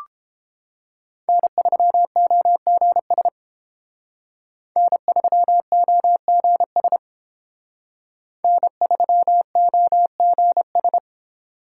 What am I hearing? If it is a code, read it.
Morse code, heard as a keyed tone, about 28 wpm, 700 Hz. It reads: N3OGS N3OGS N3OGS